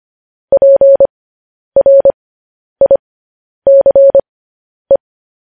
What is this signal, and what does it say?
Morse code, heard as a keyed tone, about 25 wpm, spaced as for 10 wpm, 550 Hz. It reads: PRICE
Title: News Headlines at 10